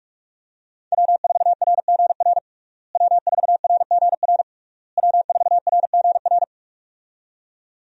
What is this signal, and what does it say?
Morse code, heard as a keyed tone, about 46 wpm, 700 Hz. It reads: W4RGR W4RGR W4RGR
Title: USA Call Signs - 5 Characters - Repeated 3x - 45wpm